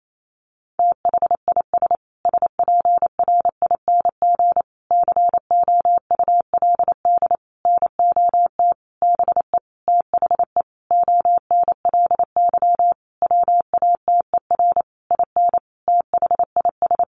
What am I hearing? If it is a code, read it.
Morse code, heard as a keyed tone, about 28 wpm, 700 Hz. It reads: THIS SPRING COULD NOT BE THE ONLY WATER IN THIS
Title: Sentences from Top 1500 Words 28wpm